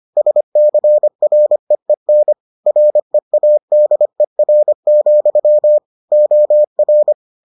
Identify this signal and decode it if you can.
Morse code, heard as a keyed tone, about 25 wpm, 600 Hz. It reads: SCREEN READER, OR